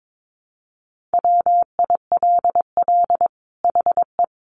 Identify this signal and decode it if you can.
Morse code, heard as a keyed tone, about 22 wpm, 700 Hz. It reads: WILL HE
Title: Sentences from Top 1400 Words 22wpm